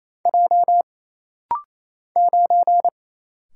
Morse code, heard as a keyed tone, about 28 wpm, 700 Hz. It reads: J 9
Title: Single Letter-Number - Rapid-Fire 28wpm